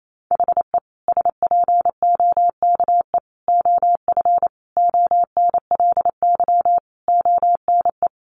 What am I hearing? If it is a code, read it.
Morse code, heard as a keyed tone, about 28 wpm, 700 Hz. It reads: HE SPOKE OF ONLY ONE
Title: Sentences from Top 1200 Words 28wpm